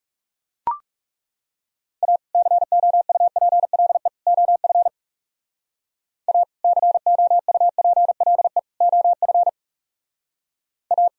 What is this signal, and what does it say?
Morse code, heard as a keyed tone, about 36 wpm, 700 Hz. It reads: A COUPLE OF A COUPLE OF A
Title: Speed-Racing - Sets of 3 Words 30wpm